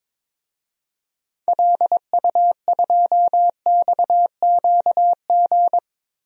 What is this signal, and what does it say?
Morse code, heard as a keyed tone, about 22 wpm, 700 Hz. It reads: LU2XQG
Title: DXCC-100 Call Signs - 6 Characters - 22wpm